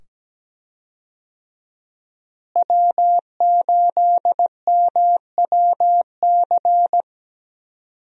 Morse code, heard as a keyed tone, about 17 wpm, 700 Hz. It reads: W8MWC
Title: USA Call Signs - 5 Characters - Repeated 2x - 17wpm